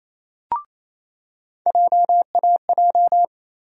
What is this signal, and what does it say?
Morse code, heard as a keyed tone, about 28 wpm, 700 Hz. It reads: JAJ
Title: USA Call Sign Postfixes - 3 Letters 28wpm